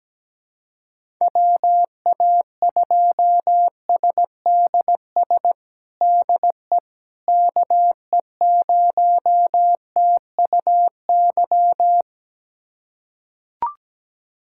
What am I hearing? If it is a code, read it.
Morse code, heard as a keyed tone, about 17 wpm, 700 Hz. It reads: WA2SDS DE KE0TUY
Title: USA Callsign DE Callsign - 6 Characters - 17wpm